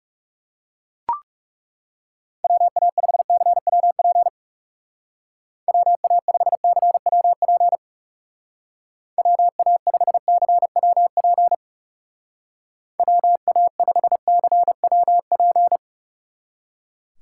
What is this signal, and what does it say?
Morse code, heard as a keyed tone, about 39 wpm, 700 Hz. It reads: WA5CWP WA5CWP WA5CWP WA5CWP